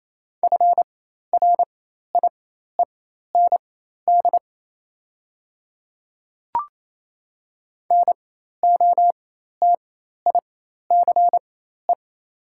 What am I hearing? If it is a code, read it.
Morse code, heard as a keyed tone, about 28 wpm, spaced as for 13 wpm, 700 Hz. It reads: FRIEND NOTICE
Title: 4x Character Spacing - Top 500 Words - 28wpm